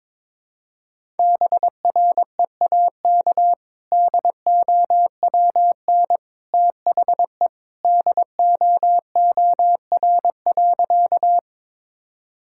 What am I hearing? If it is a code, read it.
Morse code, heard as a keyed tone, about 22 wpm, 700 Hz. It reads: BREAK DOWN THE DOOR.
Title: Sentences from Top 1000 Words - Review Part 6 - 22wpm